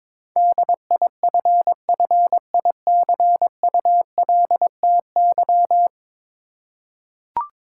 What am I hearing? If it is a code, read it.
Morse code, heard as a keyed tone, about 22 wpm, 700 Hz. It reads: DIFFICULTY
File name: practice / 22wpm / Ten Letter Words - 22wpm